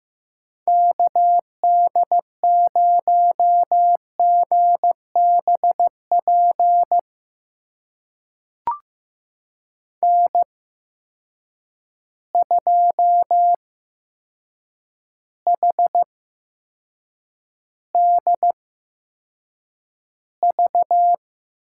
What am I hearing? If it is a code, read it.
Morse code, heard as a keyed tone, about 15 wpm, spaced as for 7 wpm, 700 Hz. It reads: KD0GBP N2HDV
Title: USA Call Signs - 8x Character Spacing - 15wpm